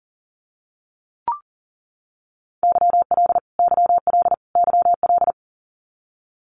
Morse code, heard as a keyed tone, about 40 wpm, 700 Hz. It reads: YL YL YL
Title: Abbreviations Repeated 3x - 40wpm